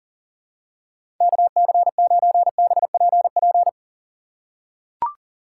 Morse code, heard as a keyed tone, about 40 wpm, 700 Hz. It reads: KC9BPP
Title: USA Call Signs - 6 Characters - 2x Character Spacing - 40wpm